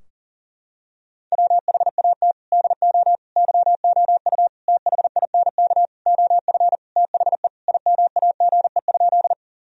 Morse code, heard as a keyed tone, about 40 wpm, 700 Hz. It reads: WHAT DO YOU THINK OF THE IMAGE?
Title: Sentences from Top 1000 Words - Review Part 4 - 40wpm